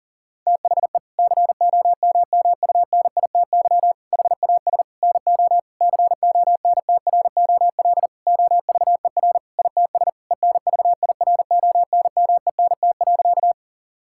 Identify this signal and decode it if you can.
Morse code, heard as a keyed tone, about 40 wpm, 700 Hz. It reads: THE COMMUNITY HAS NO CONTROL OVER ITS ENVIRONMENT.
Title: Sentences from Top 1000 Words - Review Part 5 - 40wpm